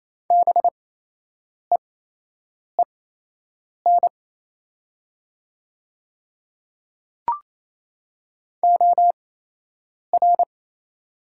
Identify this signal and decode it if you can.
Morse code, heard as a keyed tone, about 28 wpm, spaced as for 8 wpm, 700 Hz. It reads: BEEN OR
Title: 8x Character Spacing - Top 100 Words - 28wpm